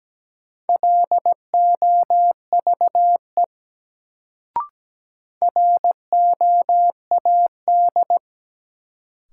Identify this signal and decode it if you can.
Morse code, heard as a keyed tone, about 17 wpm, 700 Hz. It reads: LOVE ROAD
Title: Sentences from Top 400 Words - New Word Drill - 17wpm